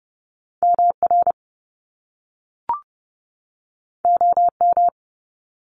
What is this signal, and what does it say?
Morse code, heard as a keyed tone, about 30 wpm, 700 Hz. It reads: MR OM
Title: Top Two Letter Combinations in Words 30wpm